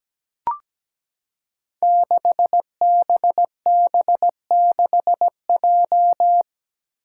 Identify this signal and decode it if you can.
Morse code, heard as a keyed tone, about 17 wpm, 700 Hz. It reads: 6BB6J
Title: Five Challenging Letter-Number Combinations 17wpm